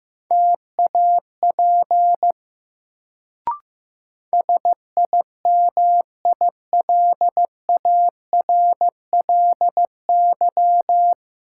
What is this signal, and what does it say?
Morse code, heard as a keyed tone, about 15 wpm, 700 Hz. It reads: TAP SIMILARLY
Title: Sentences from Top 1400 Words - New Word Drill Alt - 15wpm